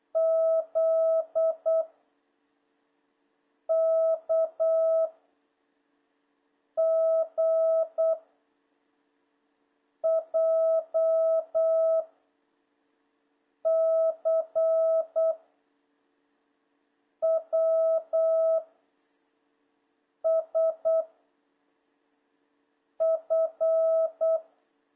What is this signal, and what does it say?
Morse code, heard as a keyed tone, about 8 wpm, 1300 Hz, with noise. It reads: Z K G J C W S F
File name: jh8cbh / 2026cw